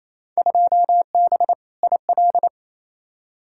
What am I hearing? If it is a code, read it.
Morse code, heard as a keyed tone, about 28 wpm, 700 Hz. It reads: 2B IL
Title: Field Day Exchanges Encore - 28wpm